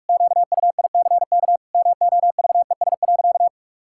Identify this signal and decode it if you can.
Morse code, heard as a keyed tone, about 48 wpm, 700 Hz. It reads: QUICK MOVES.
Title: Pangram Challenge v2 - 45wpm